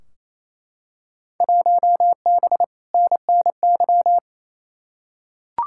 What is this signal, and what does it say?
Morse code, heard as a keyed tone, about 28 wpm, 700 Hz. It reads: 1B NNY
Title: Field Day Exchanges Encore - 28wpm